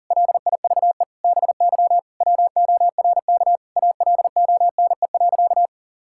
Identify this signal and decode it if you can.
Morse code, heard as a keyed tone, about 40 wpm, 700 Hz. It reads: LIVE BY WORK ALONE.
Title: Sentences from Top 1600 Words 40wpm